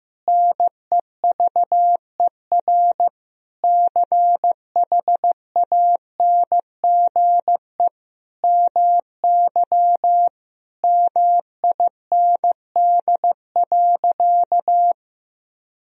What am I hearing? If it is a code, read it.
Morse code, heard as a keyed tone, about 15 wpm, 700 Hz. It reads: NEVER CHANGE MY MIND.
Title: Sentences from Top 1000 Words - Review Part 1 - 15wpm